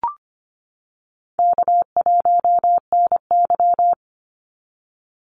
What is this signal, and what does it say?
Morse code, heard as a keyed tone, about 25 wpm, 700 Hz. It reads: K1NY
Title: Easy Call Signs - 25wpm